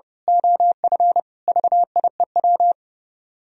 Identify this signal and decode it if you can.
Morse code, heard as a keyed tone, about 28 wpm, 700 Hz. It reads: OF VIEW
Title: Speed-Racing - Sets of 5 Words 30wpm